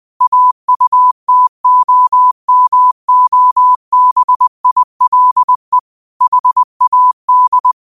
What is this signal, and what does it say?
Morse code, heard as a keyed tone, about 20 wpm, 1000 Hz. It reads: AUTOMOBILE HAD
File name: qotdfiles / 20wpm